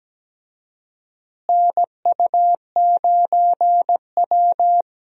Morse code, heard as a keyed tone, about 17 wpm, 700 Hz. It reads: NU9W
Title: USA Call Signs - 4 Characters - Repeated 3x - 17wpm